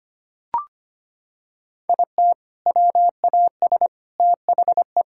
Morse code, heard as a keyed tone, about 25 wpm, 700 Hz. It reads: IT WAS THE
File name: practice / 25wpm / Sentences with 4-Word Phrases - 25wpm